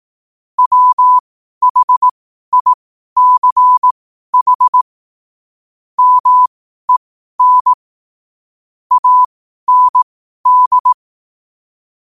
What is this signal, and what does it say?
Morse code, heard as a keyed tone, about 18 wpm, spaced as for 12 wpm, 1000 Hz. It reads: WHICH MEN AND
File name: 12wpm